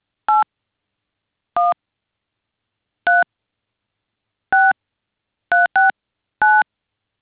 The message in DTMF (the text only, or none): 81 3 6369